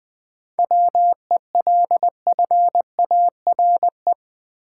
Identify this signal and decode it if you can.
Morse code, heard as a keyed tone, about 20 wpm, 700 Hz. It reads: WELFARE
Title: 7 Letter Words - First 3 Letters Given - 20wpm